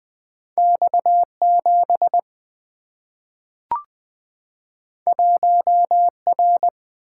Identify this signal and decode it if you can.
Morse code, heard as a keyed tone, about 20 wpm, 700 Hz. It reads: X7 1R
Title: Single-Letter Single-Number 20wpm